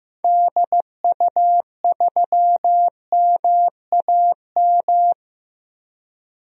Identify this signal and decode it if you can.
Morse code, heard as a keyed tone, about 15 wpm, 700 Hz. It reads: DU3MAM
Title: DXCC-100 Call Signs - 6 Characters - 15wpm